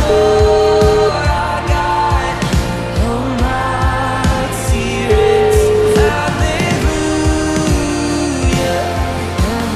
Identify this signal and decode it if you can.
Ringback tone (Europe, 425 Hz), with noise.